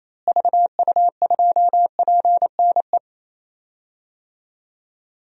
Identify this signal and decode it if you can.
Morse code, heard as a keyed tone, about 28 wpm, 700 Hz. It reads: VU2PNE
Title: DXCC-100 Call Signs - 6 Characters - Repeated 2x - 28wpm